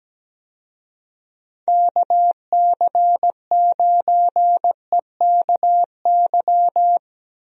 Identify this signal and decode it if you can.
Morse code, heard as a keyed tone, about 17 wpm, 700 Hz. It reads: KC9EKY
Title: USA Call Signs - 6 Characters - 17wpm